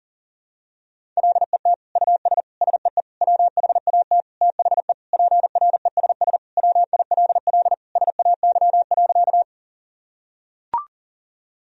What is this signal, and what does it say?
Morse code, heard as a keyed tone, about 40 wpm, 700 Hz. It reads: LET US SEE WHAT THE PRESS WILL SAY.
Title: Sentences from Top 1000 Words - Review Part 1 - 40wpm